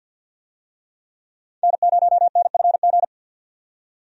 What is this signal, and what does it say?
Morse code, heard as a keyed tone, about 46 wpm, 700 Hz. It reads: N0NFG
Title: USA Call Signs - 5 Characters - 50wpm